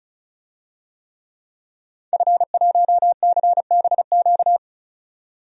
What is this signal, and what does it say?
Morse code, heard as a keyed tone, about 36 wpm, 700 Hz. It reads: F1CBQ
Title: DXCC-100 Call Signs - 5 Characters - 35wpm